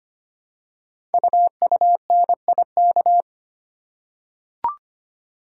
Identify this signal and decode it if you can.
Morse code, heard as a keyed tone, about 25 wpm, 700 Hz. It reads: UUNIK
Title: Code Groups - 5 Letters - 25wpm